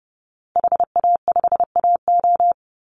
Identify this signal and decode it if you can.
Morse code, heard as a keyed tone, about 28 wpm, 700 Hz. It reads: HA5AO
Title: Call Signs Repeated 3x - 30wpm